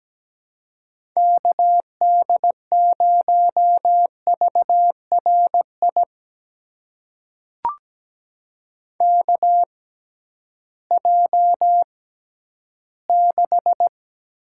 Morse code, heard as a keyed tone, about 17 wpm, spaced as for 11 wpm, 700 Hz. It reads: KD0VRI KJ6
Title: USA Call Signs - 6 Characters - 6x Character Spacing - 17wpm